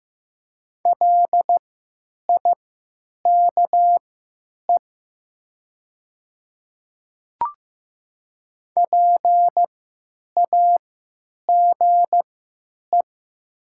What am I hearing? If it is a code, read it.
Morse code, heard as a keyed tone, about 15 wpm, spaced as for 9 wpm, 700 Hz. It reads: LIKE PAGE
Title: 3x Character Spacing - Top 300 Words - 15wpm